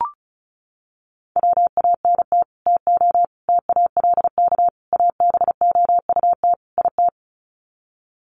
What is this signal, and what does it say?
Morse code, heard as a keyed tone, about 34 wpm, 700 Hz. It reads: WANT TO TALK ABOUT IT
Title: Sets of 5 Words Encore 35wpm